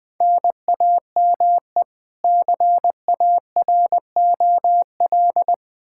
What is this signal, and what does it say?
Morse code, heard as a keyed tone, about 20 wpm, 700 Hz. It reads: NAME CAROL
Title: Practice QSO - Exchange 1 - Beginning - 20wpm